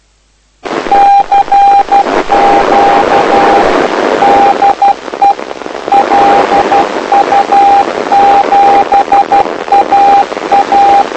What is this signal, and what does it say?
Morse code, heard as a keyed tone, about 12 wpm, 800 Hz. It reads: CQ DE LU7AA
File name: lu4aao / lu7aa 09dic2011 exp planeador vuelo2